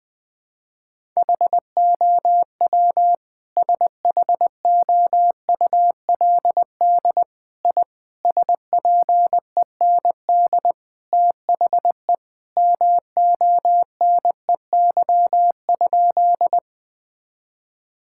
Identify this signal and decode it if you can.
Morse code, heard as a keyed tone, about 20 wpm, 700 Hz. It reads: HOW SHOULD I SPEND THE MONEY?
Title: Sentences from Top 1000 Words - Review Part 5 - 20wpm